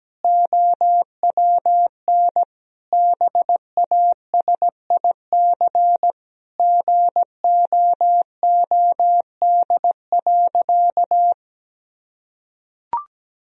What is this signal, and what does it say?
Morse code, heard as a keyed tone, about 17 wpm, 700 Hz. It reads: OWN BASIC GOOD.
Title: Sentences from Top 1000 Words - Review Part 3 - 17wpm